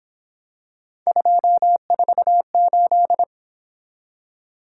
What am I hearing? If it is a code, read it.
Morse code, heard as a keyed tone, about 27 wpm, 700 Hz. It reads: 248
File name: Speed-Racing - Three Letter-Number Combinations - 22wpm